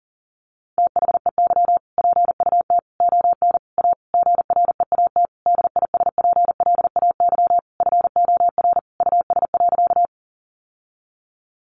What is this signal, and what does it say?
Morse code, heard as a keyed tone, about 40 wpm, 700 Hz. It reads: THEY PUT ON A GREAT DISPLAY FOR US.
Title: Sentences from Top 1200 Words 40wpm